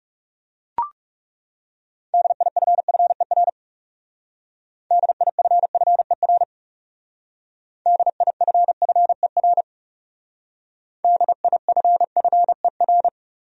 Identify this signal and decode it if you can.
Morse code, heard as a keyed tone, about 39 wpm, 700 Hz. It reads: DIFFER DIFFER DIFFER DIFFER